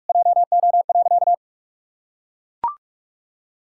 Morse code, heard as a keyed tone, about 41 wpm, 700 Hz. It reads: JO.
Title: Pangram Challenge v2 - 45wpm